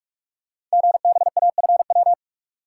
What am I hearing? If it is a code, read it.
Morse code, heard as a keyed tone, about 44 wpm, 700 Hz. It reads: GBAFW
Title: Code Groups - 5 Letters - 45wpm